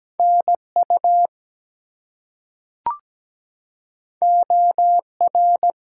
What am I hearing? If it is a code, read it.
Morse code, heard as a keyed tone, about 17 wpm, 700 Hz. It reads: NU OR
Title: US States and Canadian Provinces 17wpm